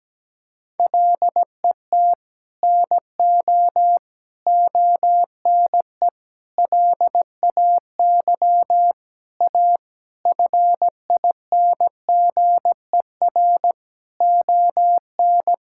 Morse code, heard as a keyed tone, about 17 wpm, 700 Hz. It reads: LET NO ONE LAY A FINGER ON
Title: Sentences from Top 1600 Words 17wpm